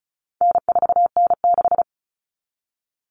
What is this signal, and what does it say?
Morse code, heard as a keyed tone, about 34 wpm, 700 Hz. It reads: N4N6
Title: Common Cut Numbers - 4-Digit 35wpm